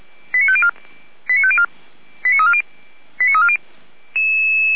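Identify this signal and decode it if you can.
Five-tone selective call (ZVEI), heard as quick runs of five tones, tones of about 70 ms. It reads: 89504 89504 89340 89340
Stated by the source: Received on SDR; ZVEI five-tone selective call, from audio.f85360.feuerwehr-orlamuende.org fire brigade server.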